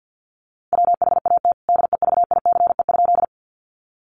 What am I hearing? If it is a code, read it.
Morse code, heard as a keyed tone, about 48 wpm, 700 Hz. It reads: WHAT DEVICE?
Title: Sentences from Top 1500 Words 50wpm